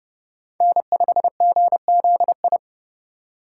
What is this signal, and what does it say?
Morse code, heard as a keyed tone, about 30 wpm, 700 Hz. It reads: N5GZI
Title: USA Call Signs - 5 Characters - 4x Character Spacing - 30wpm